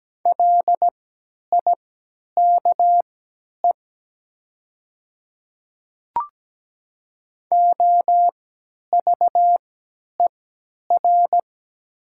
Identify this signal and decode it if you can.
Morse code, heard as a keyed tone, about 17 wpm, spaced as for 10 wpm, 700 Hz. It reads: LIKE OVER
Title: 3x Character Spacing - Top 100 Words - 17wpm